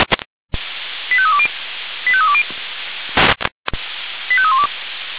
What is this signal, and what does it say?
Five-tone selective call (ZVEI), heard as quick runs of five tones, tones of about 70 ms. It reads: 85320 85320 85312